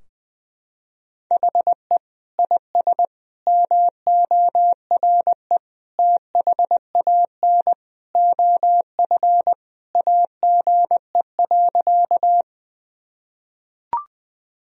Morse code, heard as a keyed tone, about 20 wpm, 700 Hz. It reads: HE IS MORE THAN OF AGE.
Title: Sentences from Top 1000 Words - Review Part 3 - 20wpm